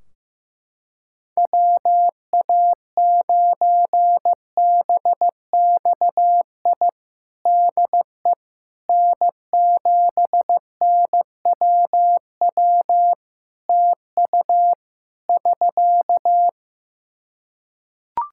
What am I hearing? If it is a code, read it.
Morse code, heard as a keyed tone, about 15 wpm, 700 Hz. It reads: WA9BXI DE N7NWW TU <SK>
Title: Practice QSO - Exchange 4 - Ending - 15wpm